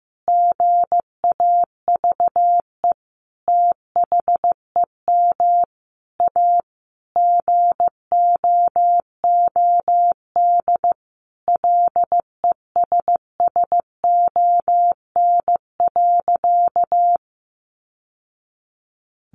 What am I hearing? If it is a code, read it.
Morse code, heard as a keyed tone, about 15 wpm, 700 Hz. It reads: GAVE THEM A GOOD LESSON.
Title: Sentences from Top 1200 Words 15wpm